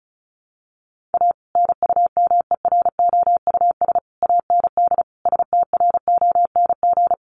Morse code, heard as a keyed tone, about 34 wpm, 700 Hz. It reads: A NUMEROUS AND STRONG
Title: Sentences from Top 1400 Words 35wpm